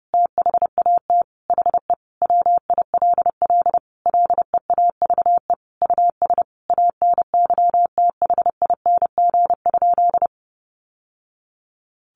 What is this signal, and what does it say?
Morse code, heard as a keyed tone, about 30 wpm, 700 Hz. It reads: THAT HE WILL LEAVE US ANYTHING?
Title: Sentences from Top 1100 Words 30wpm